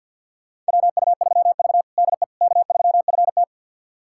Morse code, heard as a keyed tone, about 51 wpm, 700 Hz. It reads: WU3V DE K3FT
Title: USA Callsign DE Callsign - 4 Characters - 50wpm